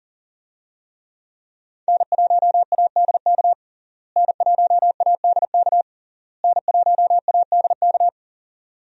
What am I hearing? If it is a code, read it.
Morse code, heard as a keyed tone, about 40 wpm, 700 Hz. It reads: N1ADK N1ADK N1ADK
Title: USA Call Signs - 5 Characters - Repeated 3x - 40wpm